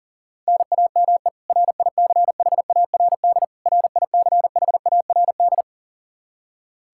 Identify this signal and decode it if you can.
Morse code, heard as a keyed tone, about 40 wpm, 700 Hz. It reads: NAME RICHARD RICHARD
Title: Practice QSO - Exchange 1 - Beginning - 40wpm